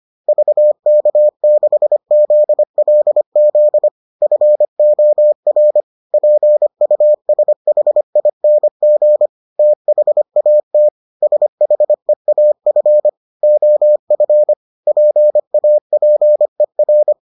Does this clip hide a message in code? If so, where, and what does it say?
Yes: Morse code from 0.3 to 17.1 s, heard as a keyed tone, about 25 wpm, 600 Hz. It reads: VK6ZLZ FOR PUSHING THAT SHEAF OF PAPER